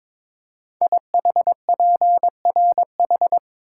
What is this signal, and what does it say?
Morse code, heard as a keyed tone, about 22 wpm, 700 Hz. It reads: IHPRH
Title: Code Groups - 5 Letters - 22wpm